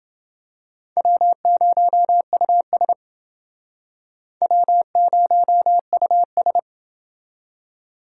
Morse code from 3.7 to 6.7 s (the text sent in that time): W0US